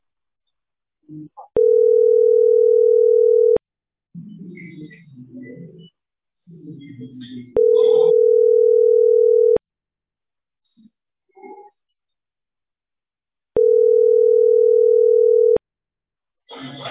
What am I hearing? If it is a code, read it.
Ringback tone (North America), 440 + 480 Hz.